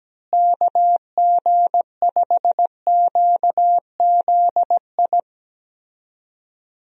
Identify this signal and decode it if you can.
Morse code, heard as a keyed tone, about 17 wpm, 700 Hz. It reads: KG5QZI
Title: USA Call Signs - 6 Characters - Repeated 3x - 17wpm